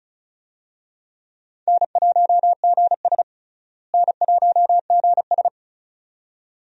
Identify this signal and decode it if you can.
Morse code, heard as a keyed tone, about 36 wpm, 700 Hz. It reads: N1GS N1GS
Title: USA Call Signs - 4 Characters - Repeated 2x - 35wpm